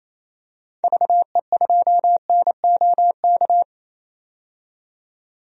Morse code, heard as a keyed tone, about 28 wpm, 700 Hz. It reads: VE2NOK
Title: DXCC-100 Call Signs - 6 Characters - 28wpm